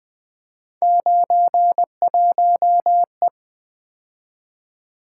Morse code, heard as a keyed tone, about 25 wpm, 700 Hz. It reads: 91E
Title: Speed-Racing - Three Letter-Number Combinations - 20wpm